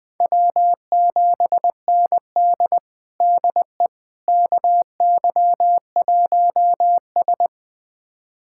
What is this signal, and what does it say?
Morse code, heard as a keyed tone, about 21 wpm, 700 Hz. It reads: W7ND DE KY1S
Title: USA Callsign DE Callsign - 4 Characters - 20wpm